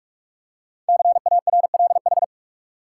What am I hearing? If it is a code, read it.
Morse code, heard as a keyed tone, about 44 wpm, 700 Hz. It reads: CARLH
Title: Code Groups - 5 Letters - 45wpm